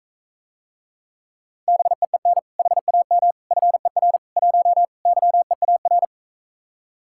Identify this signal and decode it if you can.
Morse code, heard as a keyed tone, about 45 wpm, 700 Hz. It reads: BEEN HAM FER 1 YEAR